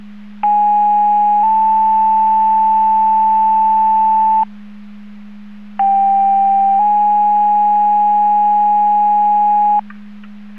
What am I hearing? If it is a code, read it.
Two-tone page, heard as a one-second tone, then a three-second tone, about 830 then 880 Hz, about 790 then 830 Hz. It is heard from 0.4 to 9.8 s.